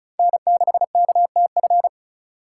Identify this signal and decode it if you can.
Morse code, heard as a keyed tone, about 36 wpm, 700 Hz. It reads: N6KTF
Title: USA Call Signs - 5 Characters - 4x Character Spacing - 35wpm